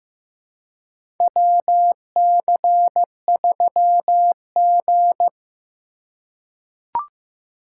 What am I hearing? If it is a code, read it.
Morse code, heard as a keyed tone, about 15 wpm, 700 Hz. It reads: WC3G
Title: USA Call Signs - 4 Characters - 2x Character Spacing - 15wpm